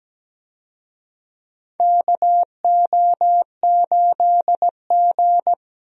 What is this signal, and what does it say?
Morse code, heard as a keyed tone, about 17 wpm, 700 Hz. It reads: KO8G
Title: USA Call Signs - 4 Characters - Repeated 2x - 17wpm